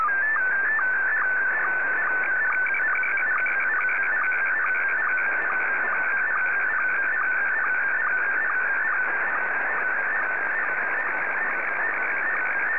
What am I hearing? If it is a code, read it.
SSTV (slow-scan television), Scottie 1.